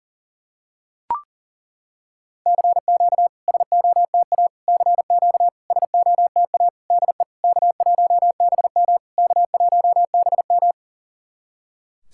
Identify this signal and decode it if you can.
Morse code, heard as a keyed tone, about 40 wpm, 690 Hz. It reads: CQ SOTA CQ SOTA DE K1BM K1BM
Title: SOTA QSO Practice 40wpm